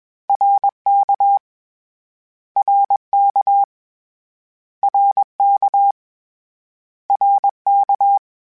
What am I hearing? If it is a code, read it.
Morse code, heard as a keyed tone, about 21 wpm, 800 Hz. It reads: RK RK RK RK
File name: Short Runs / Short Runs-mp3s / ShortContrasts @20